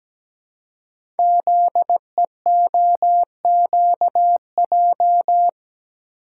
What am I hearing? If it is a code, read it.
Morse code, heard as a keyed tone, about 17 wpm, 700 Hz. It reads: ZEOQJ
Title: Code Groups - 5 Letters - 17wpm